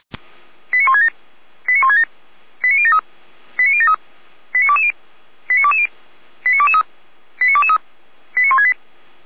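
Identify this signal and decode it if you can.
Five-tone selective call (ZVEI), heard as quick runs of five tones, tones of about 70 ms. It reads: 89167 89167 89073 89073 89220 89220 89223 89223 89168
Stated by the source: ZVEI five-tone selective call received on an SDR.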